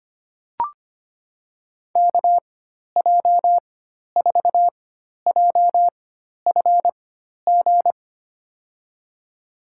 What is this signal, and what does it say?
Morse code, heard as a keyed tone, about 25 wpm, spaced as for 12 wpm, 700 Hz. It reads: KJ4JFG
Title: USA Call Signs - 6 Characters - 4x Character Spacing - 25wpm